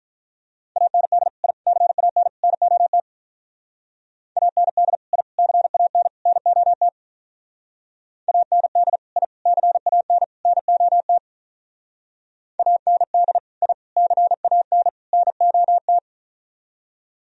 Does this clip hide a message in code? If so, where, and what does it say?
Yes: Morse code from 0.8 to 16.0 s, heard as a keyed tone, about 44 wpm, 700 Hz. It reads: AND I CAN NOT AND I CAN NOT AND I CAN NOT AND I CAN NOT